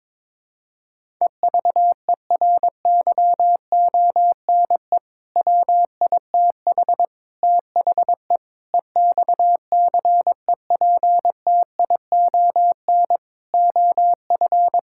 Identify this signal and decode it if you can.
Morse code, heard as a keyed tone, about 22 wpm, 700 Hz. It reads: EVERYONE WITH THE EXCEPTION OF
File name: practice / 22wpm / Sentences with 4-Word Phrases - 22wpm